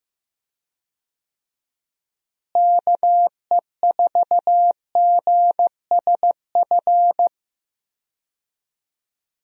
Morse code, heard as a keyed tone, about 16 wpm, 700 Hz. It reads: KE4GSF
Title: USA Call Signs - 6 Characters - Repeated 2x - 15wpm